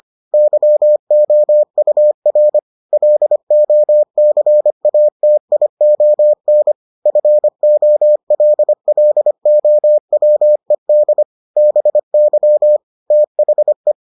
Morse code, heard as a keyed tone, about 25 wpm, 600 Hz. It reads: YOUR LOCATION FOLLOWED BY THE